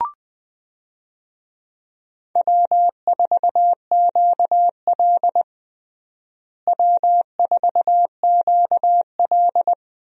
Morse code, heard as a keyed tone, about 20 wpm, 700 Hz. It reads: W4QL W4QL
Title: USA Call Signs - 4 Characters - Repeated 3x - 20wpm